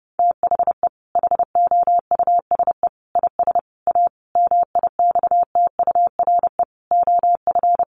Morse code, heard as a keyed tone, about 30 wpm, 700 Hz. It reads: THE HOUSE IS A MIXTURE OF
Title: Sentences from Top 1500 Words 30wpm